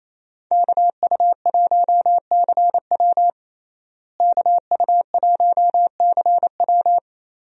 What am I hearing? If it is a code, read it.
Morse code, heard as a keyed tone, about 28 wpm, 700 Hz. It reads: KU1CW KU1CW
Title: USA Call Signs - 5 Characters - Repeated 3x - 28wpm